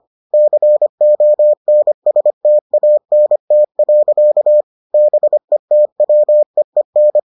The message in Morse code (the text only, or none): CONSTANT. BETWEEN